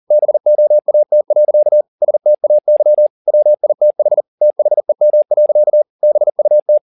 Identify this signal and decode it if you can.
Morse code, heard as a keyed tone, about 40 wpm, 600 Hz. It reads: BOAT. STAY WITH THEM. BUT
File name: the old man and the sea 1 40wpm